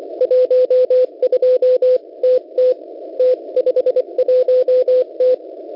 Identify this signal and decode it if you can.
Morse code, heard as a keyed tone, about 24 wpm, 510 Hz. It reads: 12TT T51T
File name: Unid- -0700utc-4735khz